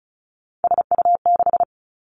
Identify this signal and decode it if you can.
Morse code, heard as a keyed tone, about 36 wpm, 700 Hz. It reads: SU6
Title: Three Challenging Letter-Number Combinations 35wpm